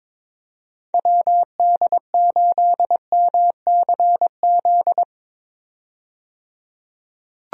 Morse code, heard as a keyed tone, about 22 wpm, 700 Hz. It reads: WD8MCZ
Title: USA Call Signs - 6 Characters - 22wpm